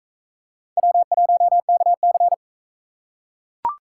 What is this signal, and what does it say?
Morse code, heard as a keyed tone, about 40 wpm, 700 Hz. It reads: W1KC